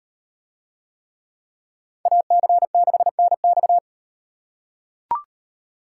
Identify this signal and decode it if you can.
Morse code, heard as a keyed tone, about 38 wpm, 700 Hz. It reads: AC6NX